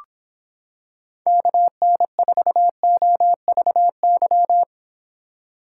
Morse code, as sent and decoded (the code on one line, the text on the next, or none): -.- -. ....- --- ...- -.--
KN4OVY